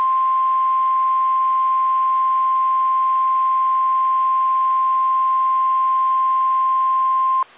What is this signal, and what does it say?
Emergency Alert System (SAME) tones (the 1050 Hz attention tone). It is heard from 1.2 to 6.9 s.
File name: eas / KSDZKDJL / KSDZKDJL